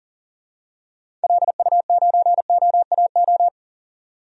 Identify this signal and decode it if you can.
Morse code, heard as a keyed tone, about 40 wpm, 700 Hz. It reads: LU9OAO
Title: DXCC-100 Call Signs - 6 Characters - 40wpm